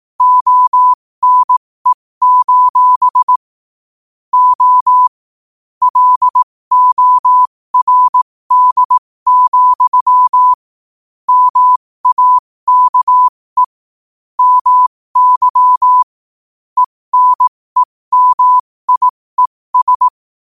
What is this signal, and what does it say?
Morse code, heard as a keyed tone, about 18 wpm, spaced as for 15 wpm, 1000 Hz. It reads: ONE: O LORD, MAKE MY ENEMIES
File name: qotdfiles / 15wpm